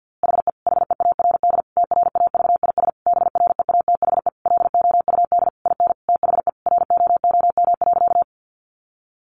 Morse code, heard as a keyed tone, about 50 wpm, 700 Hz. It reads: HE HEARD TRAVIS BREATHE LOUD IN THE ROOM.
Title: A Sound Of Thunder 50wpm